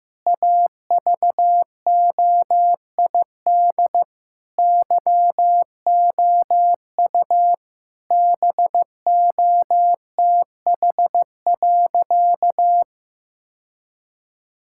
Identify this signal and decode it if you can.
Morse code, heard as a keyed tone, about 15 wpm, 700 Hz. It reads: AVOID YOU BOTH.
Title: Sentences from Top 1000 Words - Review Part 6 - 15wpm